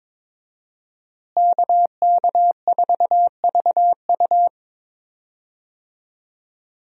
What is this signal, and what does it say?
Morse code, heard as a keyed tone, about 22 wpm, 700 Hz. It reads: KK4VU
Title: USA Call Signs - 5 Characters - Repeated 2x - 22wpm